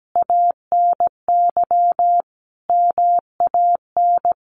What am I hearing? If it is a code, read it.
Morse code, heard as a keyed tone, about 17 wpm, 700 Hz. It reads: ANY MAN
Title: Sentences from Top 1500 Words 17wpm